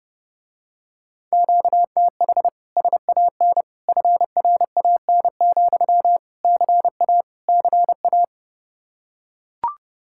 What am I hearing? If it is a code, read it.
Morse code, heard as a keyed tone, about 31 wpm, 700 Hz. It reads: QTH SAN FRAN, CA CA